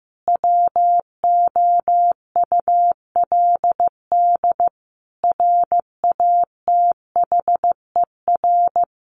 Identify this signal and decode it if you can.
Morse code, heard as a keyed tone, about 15 wpm, 700 Hz. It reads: WOULD RATHER
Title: Sentences from Top 700 Words 15wpm